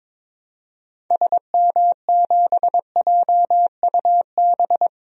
Morse code, heard as a keyed tone, about 22 wpm, 700 Hz. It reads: SM7JUB
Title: DXCC-100 Call Signs - 6 Characters - Repeated 2x - 22wpm